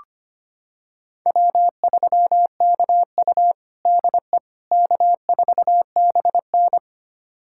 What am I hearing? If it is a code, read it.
Morse code, heard as a keyed tone, about 25 wpm, 700 Hz. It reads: W3KU DE K4BN
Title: USA Callsign DE Callsign - 4 Characters - 25wpm